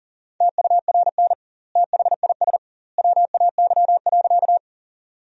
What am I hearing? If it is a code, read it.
Morse code, heard as a keyed tone, about 40 wpm, 700 Hz. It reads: TURN THIS WAY.
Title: 2x Word-Spacing - Sentences from Top 200 Words - 40wpm